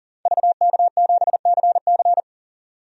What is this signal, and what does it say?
Morse code, heard as a keyed tone, about 40 wpm, 700 Hz. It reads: VK7CC
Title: DXCC-100 Call Signs - 5 Characters - 40wpm